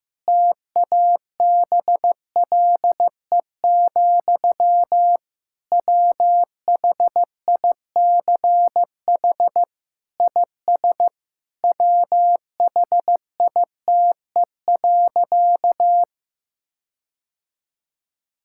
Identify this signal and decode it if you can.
Morse code, heard as a keyed tone, about 15 wpm, 700 Hz. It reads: TABLE, WHICH IS WHITE.
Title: Sentences from Top 1600 Words 15wpm